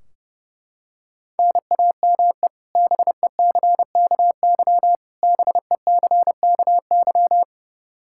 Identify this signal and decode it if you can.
Morse code, heard as a keyed tone, about 32 wpm, 700 Hz. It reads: NAME BECKY BECKY
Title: K1USN SST QSO Practice 30wpm